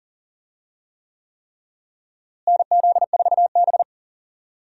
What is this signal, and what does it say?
Morse code, heard as a keyed tone, about 40 wpm, 700 Hz. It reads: NZ4B
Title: USA Call Signs - 4 Characters - Repeated 3x - 40wpm